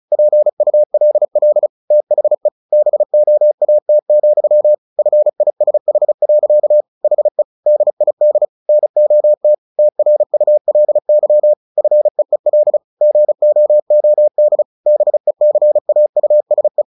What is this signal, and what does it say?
Morse code, heard as a keyed tone, about 34 wpm, 600 Hz. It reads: PULL THE BOAT, FISH. HE DID NOT TRULY FEEL GOOD BECAUSE
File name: the old man and the sea 2 35wpm